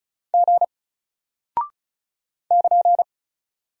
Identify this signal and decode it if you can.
Morse code, heard as a keyed tone, about 36 wpm, 700 Hz. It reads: G <KN>
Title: Single Character (Letter:Number:Punctuation:Prosign) Alt 35wpm